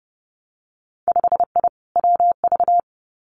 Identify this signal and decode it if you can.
Morse code, heard as a keyed tone, about 30 wpm, 700 Hz. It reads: 5I WV
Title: Winter Field Day Exchanges Encore 30wpm